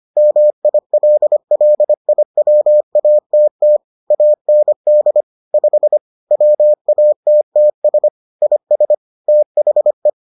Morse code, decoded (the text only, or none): MILLIWATT AND 5 WATTS IS THE